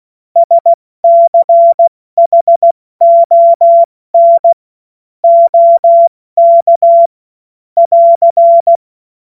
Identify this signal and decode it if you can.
Morse code, heard as a keyed tone, about 16 wpm, spaced as for 14 wpm, 680 Hz. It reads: SCHON OK <AR>
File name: Morsekurs / Witze / Witz 13. Juli 2021, 70 F80